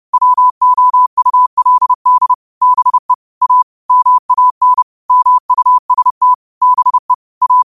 Morse code, heard as a keyed tone, about 30 wpm, 1000 Hz. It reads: WOULD BE A MAN MUST BE A